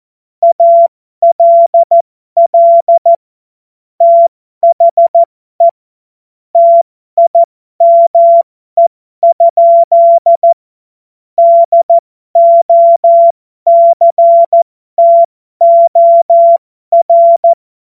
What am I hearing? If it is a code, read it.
Morse code, heard as a keyed tone, about 14 wpm, spaced as for 12 wpm, 680 Hz. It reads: ALL THE TIME? DOCTOR
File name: Morsekurs / Witze